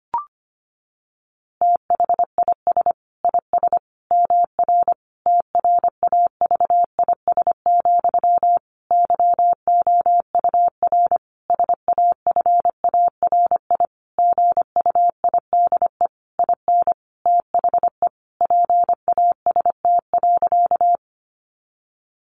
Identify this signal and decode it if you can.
Morse code, heard as a keyed tone, about 25 wpm, 700 Hz. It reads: THIS IS MR TRAVIS, YOUR SAFARI GUIDE IN THE PAST.
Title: A Sound Of Thunder 25wpm